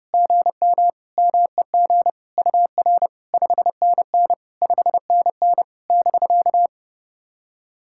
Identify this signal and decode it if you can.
Morse code, heard as a keyed tone, about 28 wpm, 700 Hz. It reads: GM MEG UR 5NN 5NN <BK>
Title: SOTA QSO Practice 30wpm